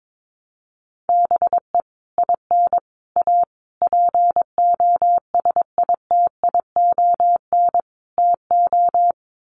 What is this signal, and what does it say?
Morse code, heard as a keyed tone, about 22 wpm, 700 Hz. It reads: BE IN A POSITION TO
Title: Sets of 5 Words Encore2 22wpm